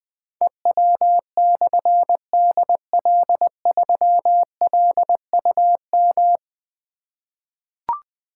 Morse code, heard as a keyed tone, about 20 wpm, 700 Hz. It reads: EW/DL3LUM
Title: Difficult Call Signs 20wpm